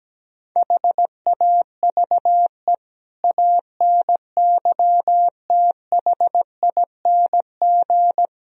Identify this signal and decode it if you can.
Morse code, heard as a keyed tone, about 17 wpm, 700 Hz. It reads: HAVE ANYTHING
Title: Speed-Racing - Sets of 5 Words 17wpm